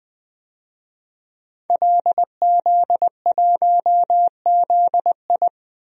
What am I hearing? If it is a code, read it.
Morse code, heard as a keyed tone, about 21 wpm, 700 Hz. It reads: LZ1ZI
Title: DXCC-100 Call Signs - 5 Characters - 20wpm